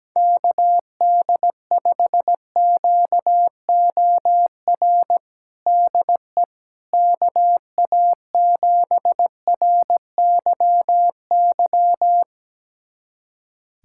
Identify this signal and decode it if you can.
Morse code, heard as a keyed tone, about 17 wpm, 700 Hz. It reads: KD5QOR DE KA7RYY
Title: USA Callsign DE Callsign - 6 Characters - 17wpm